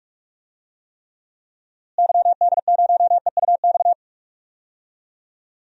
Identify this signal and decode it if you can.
Morse code, heard as a keyed tone, about 46 wpm, 700 Hz. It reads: YD0EUX
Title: DXCC-100 Call Signs - 6 Characters - Repeated 3x - 45wpm